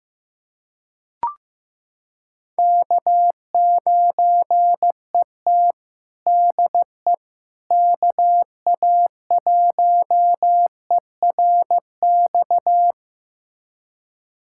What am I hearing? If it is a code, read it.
Morse code, heard as a keyed tone, about 15 wpm, 700 Hz. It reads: K9ET DE KA1ERX
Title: USA Callsign DE Callsign 15wpm